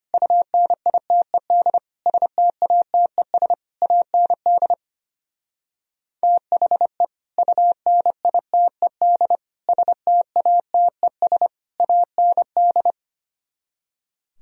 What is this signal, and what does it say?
Morse code, heard as a keyed tone, about 32 wpm, 700 Hz. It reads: UNITED STATES AND THE UNITED STATES AND